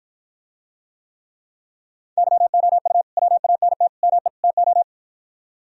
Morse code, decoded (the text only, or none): YOU WANT ME TO